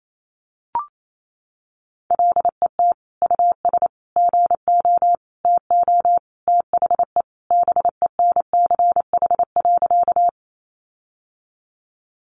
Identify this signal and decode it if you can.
Morse code, heard as a keyed tone, about 28 wpm, 700 Hz. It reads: LET US GO TO THE BENCH.
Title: Sentences from Top 1400 Words 28wpm